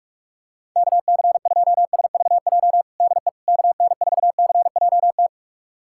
Morse code, heard as a keyed tone, about 41 wpm, 700 Hz. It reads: KC2SVJ DE KN4CJT
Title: USA Callsign DE Callsign - 6 Characters - 45wpm